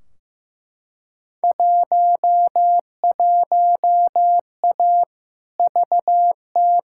Morse code, heard as a keyed tone, about 15 wpm, 700 Hz. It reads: 11A VT
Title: Field Day Exchanges Encore - 15wpm